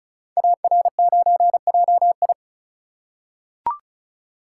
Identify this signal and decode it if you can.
Morse code, heard as a keyed tone, about 36 wpm, 690 Hz. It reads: AR9JI
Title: Code Groups - 5 Letter:Numbers - 35wpm